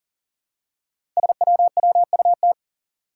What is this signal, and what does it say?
Morse code, heard as a keyed tone, about 42 wpm, 700 Hz. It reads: SWWUT
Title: Five Challenging Letter-Number Combinations 40wpm